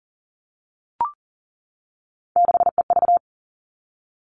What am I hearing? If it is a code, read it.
Morse code, heard as a keyed tone, about 40 wpm, 700 Hz. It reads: 6EV
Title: All Cut Numbers - 3-Digit 40wpm